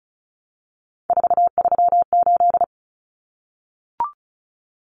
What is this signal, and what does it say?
Morse code, heard as a keyed tone, about 34 wpm, 700 Hz. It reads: 438
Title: Common Cut Numbers - 3-Digit 35wpm